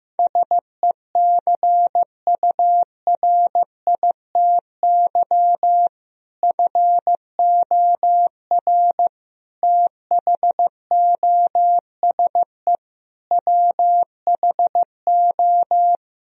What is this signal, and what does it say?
Morse code, heard as a keyed tone, about 15 wpm, 700 Hz. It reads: SECURITY FOR THOSE WHO
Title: Sentences from Top 1000 Words - Review Part 4 - 15wpm